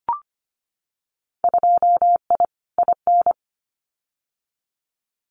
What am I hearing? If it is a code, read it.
Morse code, heard as a keyed tone, about 25 wpm, 700 Hz. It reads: 2I IN
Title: Winter Field Day Exchanges Encore 25wpm